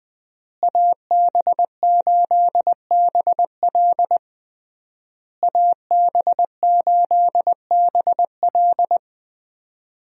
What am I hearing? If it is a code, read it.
Morse code, heard as a keyed tone, about 20 wpm, 700 Hz. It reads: AB8BL AB8BL
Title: USA Call Signs - 5 Characters - Repeated 3x - 20wpm